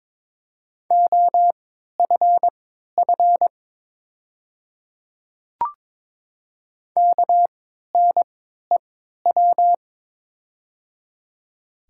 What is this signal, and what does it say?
Morse code, heard as a keyed tone, about 22 wpm, spaced as for 13 wpm, 700 Hz. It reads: OFF KNEW
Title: 3x Character Spacing - Top 500 Words - 22wpm